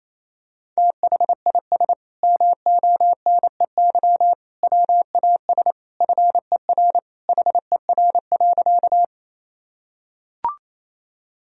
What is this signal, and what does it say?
Morse code, heard as a keyed tone, about 28 wpm, 700 Hz. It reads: THIS MONEY WAS FER HER.
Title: Sentences from Top 400 Words and Abbreviations 28wpm